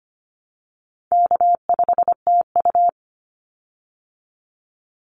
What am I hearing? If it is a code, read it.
Morse code, heard as a keyed tone, about 25 wpm, 700 Hz. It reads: K5TU
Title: Easy Call Signs Repeated 2x - 25wpm